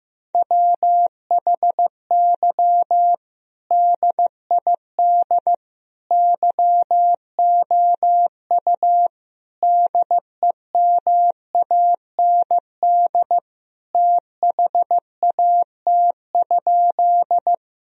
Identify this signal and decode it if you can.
Morse code, heard as a keyed tone, about 16 wpm, 700 Hz. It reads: WHY DID YOU DEMAND THAT?
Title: Sentences from Top 1000 Words - Review Part 4 - 15wpm